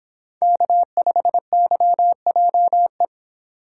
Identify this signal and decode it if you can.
Morse code, heard as a keyed tone, about 25 wpm, 700 Hz. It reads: K5YJE